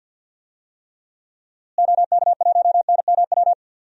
Morse code, heard as a keyed tone, about 52 wpm, 700 Hz. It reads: OK1NMW
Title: DXCC-100 Call Signs - 6 Characters - Repeated 3x - 50wpm